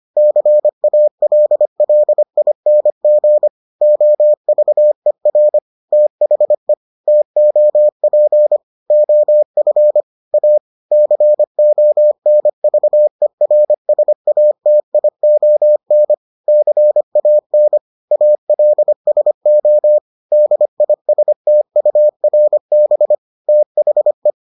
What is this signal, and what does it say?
Morse code, heard as a keyed tone, about 25 wpm, 600 Hz. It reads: CALLING OVER THE TOP OF A CONVERSATION CAN ALSO DISTURB THE